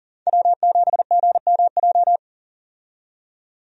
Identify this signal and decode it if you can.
Morse code, heard as a keyed tone, about 40 wpm, 700 Hz. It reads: W7GMJ
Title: USA Call Signs - 5 Characters - 40wpm